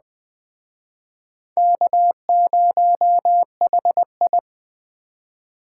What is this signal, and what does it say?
Morse code, heard as a keyed tone, about 22 wpm, 700 Hz. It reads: K0HI